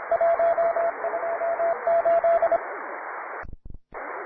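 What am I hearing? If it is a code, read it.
Morse code, heard as a keyed tone, about 27 wpm, 670 Hz, with radio noise. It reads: 128
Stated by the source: Morse code QTC list traffic from the Worked All Europe (WAE) contest, advanced level.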